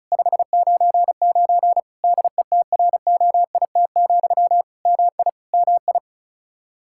Morse code, heard as a keyed tone, about 34 wpm, 700 Hz. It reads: 599 DETROIT, MI MI
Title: Practice QSO - Exchange 1 - Beginning - 35wpm